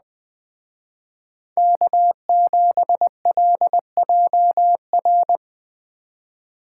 Morse code, heard as a keyed tone, about 21 wpm, 700 Hz. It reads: K7LJR
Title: Speed Racing - USA 1x3 Call Signs 17wpm